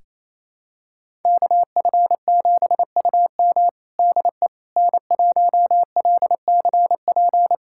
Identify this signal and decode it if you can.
Morse code, heard as a keyed tone, about 28 wpm, 700 Hz. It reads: KF7UM DE N1LCP
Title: USA Callsign DE Callsign - 5 Characters - 28wpm